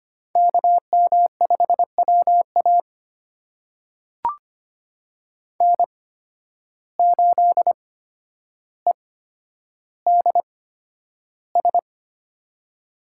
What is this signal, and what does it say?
Morse code, heard as a keyed tone, about 25 wpm, spaced as for 11 wpm, 700 Hz. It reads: KM5WA N8EDS
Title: USA Call Signs - 5 Characters - 8x Character Spacing - 25wpm